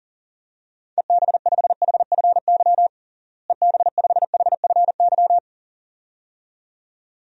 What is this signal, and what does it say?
Morse code, heard as a keyed tone, about 37 wpm, 700 Hz. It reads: EB5HFY EB5HFY
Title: DXCC-100 Call Signs - 6 Characters - Repeated 2x - 40wpm